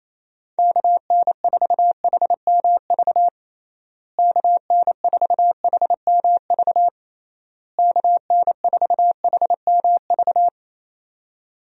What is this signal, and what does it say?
Morse code, heard as a keyed tone, about 27 wpm, 700 Hz. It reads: KN4HMV KN4HMV KN4HMV
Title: USA Call Signs - 6 Characters - Repeated 3x - 28wpm